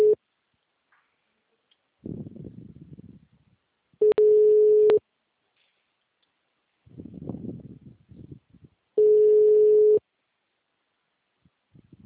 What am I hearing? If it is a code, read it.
Ringback tone (Europe, 425 Hz).